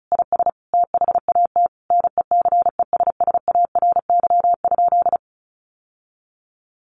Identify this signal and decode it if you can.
Morse code, heard as a keyed tone, about 34 wpm, 700 Hz. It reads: IS THAT NECESSARY?
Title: Sentences from Top 800 Words 35wpm